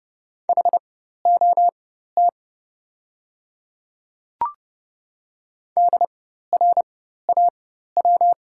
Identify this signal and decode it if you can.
Morse code, heard as a keyed tone, about 30 wpm, spaced as for 14 wpm, 690 Hz. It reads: HOT DRAW
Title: 4x Character Spacing - Top 500 Words - 30wpm